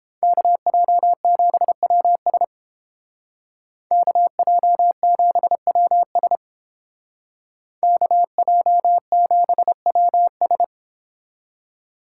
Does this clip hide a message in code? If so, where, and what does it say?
Yes: Morse code from 0.2 to 10.6 s, heard as a keyed tone, about 30 wpm, 690 Hz. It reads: KJ7WS KJ7WS KJ7WS